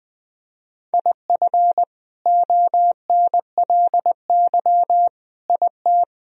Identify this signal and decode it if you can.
Morse code, heard as a keyed tone, about 20 wpm, 700 Hz. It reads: IF ONLY IT
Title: Sentences from Top 500 Words and Abbreviations 20wpm